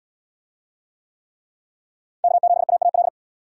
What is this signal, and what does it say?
Morse code, heard as a keyed tone, about 65 wpm, 700 Hz. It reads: K7AAL